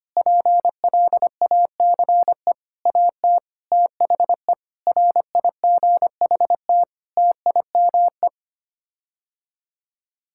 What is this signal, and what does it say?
Morse code, heard as a keyed tone, about 25 wpm, 700 Hz. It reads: PLACE AT THE RIGHT TIME
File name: Sets of 9 Words - 25wpm